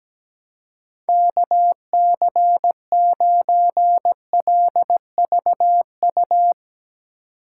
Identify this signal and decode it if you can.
Morse code, heard as a keyed tone, about 17 wpm, 700 Hz. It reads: KC9LVU
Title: USA Call Signs - 6 Characters - 4x Character Spacing - 17wpm